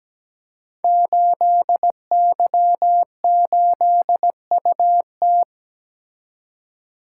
Morse code, heard as a keyed tone, about 17 wpm, 700 Hz. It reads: 8Y8UT
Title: Code Groups - 5 Letter:Numbers - 17wpm